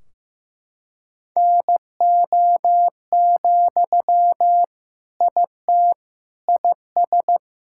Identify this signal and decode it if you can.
Morse code, heard as a keyed tone, about 16 wpm, 700 Hz. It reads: NO, IT IS
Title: Sentences from Top 1600 Words 15wpm